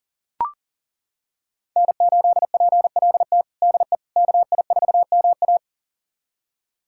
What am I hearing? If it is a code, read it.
Morse code, heard as a keyed tone, about 40 wpm, 700 Hz. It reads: N8PLT DE KI4MA
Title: USA Callsign DE Callsign - 5 Characters - 40wpm